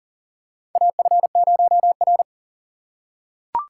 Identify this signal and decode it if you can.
Morse code, heard as a keyed tone, about 40 wpm, 700 Hz. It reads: AF0R
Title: USA Call Signs - 3x Character Spacing - 40wpm